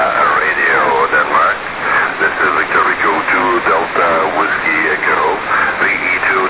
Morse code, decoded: none